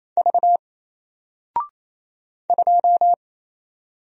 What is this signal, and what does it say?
Morse code, heard as a keyed tone, about 27 wpm, 700 Hz. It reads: V 2
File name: Single-Letter-Number-28wpm